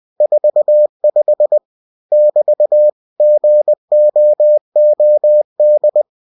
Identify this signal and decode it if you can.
Morse code, heard as a keyed tone, about 20 wpm, 600 Hz. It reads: 45 <BT> GOOD
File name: cw-books / winnie-the-pooh / winnie-20wpm-page-0045